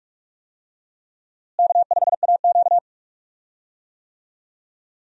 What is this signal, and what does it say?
Morse code, heard as a keyed tone, about 41 wpm, 700 Hz. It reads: K5AQ
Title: USA Call Signs - 4 Characters - Repeated 2x - 45wpm